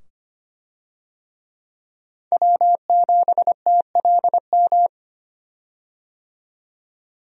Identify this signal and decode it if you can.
Morse code, heard as a keyed tone, about 25 wpm, 700 Hz. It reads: W7TLM
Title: USA Call Signs - 5 Characters - 25wpm